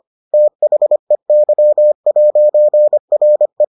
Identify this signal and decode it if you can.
Morse code, heard as a keyed tone, about 25 wpm, 600 Hz. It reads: THEY'RE